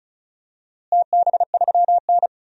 Morse code, heard as a keyed tone, about 34 wpm, 700 Hz. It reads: TB3N
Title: DXCC-100 Call Signs - 4 Characters - Repeated 2x - 35wpm